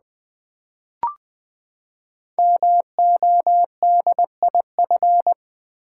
Morse code, heard as a keyed tone, about 20 wpm, 700 Hz. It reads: MODIF
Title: Build A Word - 5 Letters At A Time - 12-Letter Words - 20wpm